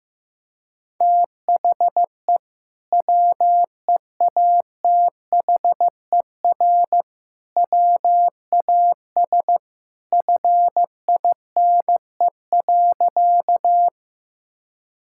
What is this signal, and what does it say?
Morse code, heard as a keyed tone, about 15 wpm, 700 Hz. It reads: THE WEATHER WAS FINE.
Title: Sentences from Top 1000 Words - Review Part 5 - 15wpm